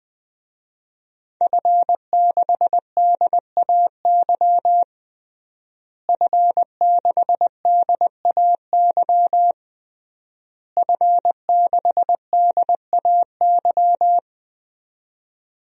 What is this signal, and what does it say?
Morse code, heard as a keyed tone, about 20 wpm, 700 Hz. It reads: F6DAY F6DAY F6DAY
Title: DXCC-100 Call Signs - 5 Characters - Repeated 3x - 20wpm